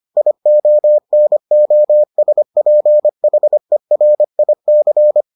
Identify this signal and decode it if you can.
Morse code, heard as a keyed tone, about 25 wpm, 600 Hz. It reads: IONOSPHERIC